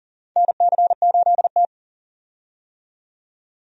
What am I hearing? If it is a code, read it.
Morse code, heard as a keyed tone, about 40 wpm, 700 Hz. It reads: NC8T
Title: USA Call Signs - 4 Characters - 40wpm